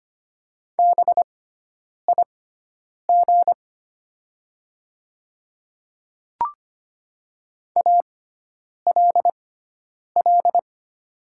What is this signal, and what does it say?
Morse code, heard as a keyed tone, about 25 wpm, spaced as for 9 wpm, 690 Hz. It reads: BIG ALL
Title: 6x Character Spacing - Top 200 Words - 25wpm